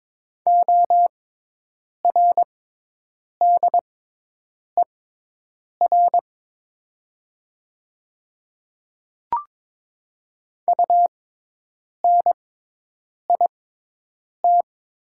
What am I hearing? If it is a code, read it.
Morse code, heard as a keyed tone, about 22 wpm, spaced as for 8 wpm, 700 Hz. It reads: ORDER UNIT
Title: 6x Character Spacing - Top 500 Words - 22wpm